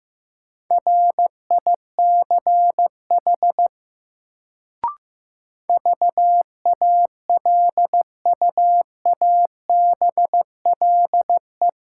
Morse code, heard as a keyed tone, about 16 wpm, 700 Hz. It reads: RICH VALUABLE
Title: Sentences from Top 1100 Words - New Word Drill - 15wpm